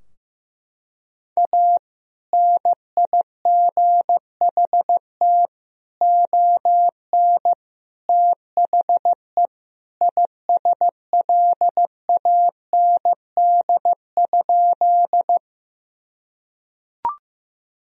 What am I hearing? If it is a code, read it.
Morse code, heard as a keyed tone, about 15 wpm, 700 Hz. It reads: A NIGHT ON THE ISLAND?
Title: Sentences from Top 1000 Words - Review Part 3 - 15wpm